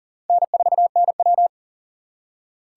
Morse code, heard as a keyed tone, about 40 wpm, 700 Hz. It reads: N4NW
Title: K1USN SST QSO Practice - 40wpm